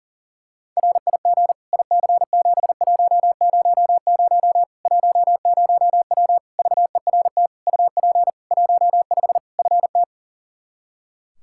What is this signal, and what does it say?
Morse code, heard as a keyed tone, about 40 wpm, 700 Hz. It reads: RIG IC7100 10W VERT UP 15 FT
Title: Practice QSO - Exchange 2 - Info - 40wpm